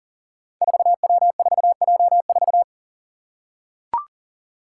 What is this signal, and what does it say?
Morse code, heard as a keyed tone, about 40 wpm, 700 Hz. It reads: 4W4J4
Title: Five Challenging Letter-Number Combinations 40wpm